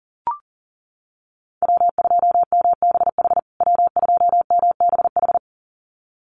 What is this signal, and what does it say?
Morse code, heard as a keyed tone, about 40 wpm, 700 Hz. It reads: W2MBH W2MBH
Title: USA Call Signs Repeated 2x 40wpm